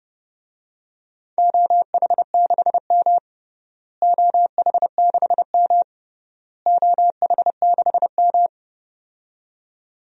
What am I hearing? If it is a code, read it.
Morse code, heard as a keyed tone, about 30 wpm, 700 Hz. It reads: OH6M OH6M OH6M
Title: DXCC-100 Call Signs - 4 Characters - Repeated 3x - 30wpm